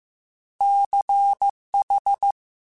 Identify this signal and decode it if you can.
Morse code, heard as a keyed tone, about 15 wpm, 800 Hz. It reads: CH